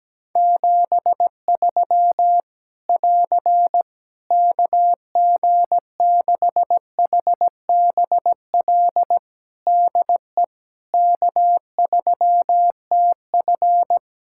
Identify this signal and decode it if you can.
Morse code, heard as a keyed tone, about 17 wpm, 700 Hz. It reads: 73 <AR> KG6HBL DE K3TF
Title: Practice QSO - Exchange 4 - Ending - 17wpm